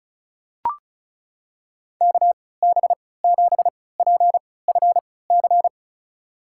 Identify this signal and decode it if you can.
Morse code, heard as a keyed tone, about 36 wpm, spaced as for 20 wpm, 700 Hz. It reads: KB7PFC
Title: USA Call Signs - 6 Characters - 3x Character Spacing - 35wpm